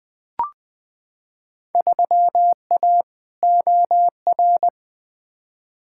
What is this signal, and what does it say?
Morse code, heard as a keyed tone, about 21 wpm, 700 Hz. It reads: 3A OR
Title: Field Day Exchanges - 20wpm